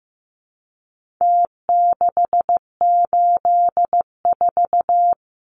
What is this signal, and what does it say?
Morse code, heard as a keyed tone, about 15 wpm, 700 Hz. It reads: T684
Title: Numbers - Contest Style - 4-Digit 15wpm